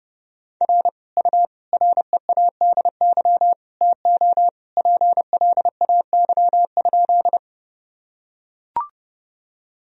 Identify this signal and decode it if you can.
Morse code, heard as a keyed tone, about 30 wpm, 700 Hz. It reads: R U READY TO PLAY?
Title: Sentences from Top 400 Words and Abbreviations 30wpm